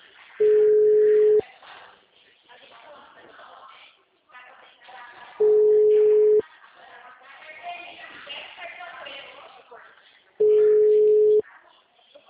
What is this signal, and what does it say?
Ringback tone (Europe, 425 Hz), with speech.
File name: out- -2029- - - .38267